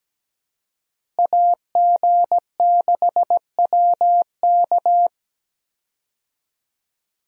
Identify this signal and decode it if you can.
Morse code, heard as a keyed tone, about 17 wpm, 700 Hz. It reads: AG6WK
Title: USA Call Signs - 5 Characters - Repeated 3x - 17wpm